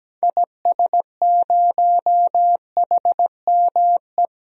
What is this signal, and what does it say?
Morse code, heard as a keyed tone, about 17 wpm, 700 Hz. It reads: IS0HME
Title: DXCC-100 Call Signs - 6 Characters - 17wpm